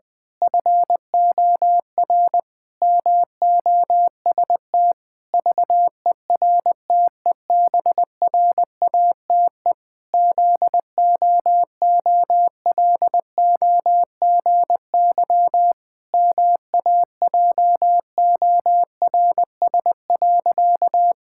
Morse code, heard as a keyed tone, about 20 wpm, 700 Hz. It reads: FOR MOST VERTEBRATE ZOOLOGY MAJORS.
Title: Pangram Challenge v2 - 20wpm